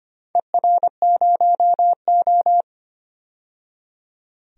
Morse code, heard as a keyed tone, about 25 wpm, 700 Hz. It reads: ER0O
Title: DXCC-100 Call Signs - 4 Characters - 25wpm